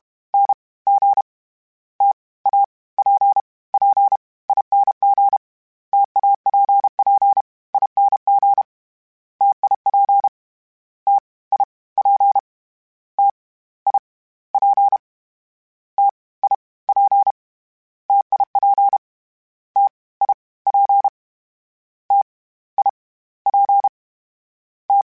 Morse code, heard as a keyed tone, about 32 wpm, spaced as for 20 wpm, 800 Hz. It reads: NG TAPPING TAPPING TIP TIP TIP TIP TIP TIP TIP T
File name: Difficult Combos / Difficult Combos-mp3s / T AOEI @30